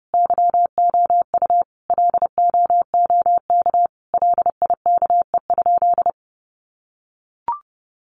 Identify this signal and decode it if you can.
Morse code, heard as a keyed tone, about 30 wpm, 700 Hz. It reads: YOU LOOK LIKE?
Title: Sentences from Top 100 Words 30wpm